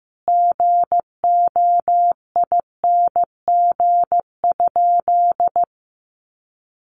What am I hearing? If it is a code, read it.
Morse code, heard as a keyed tone, about 15 wpm, 700 Hz. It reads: GOING?
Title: Sentences from Top 1100 Words 15wpm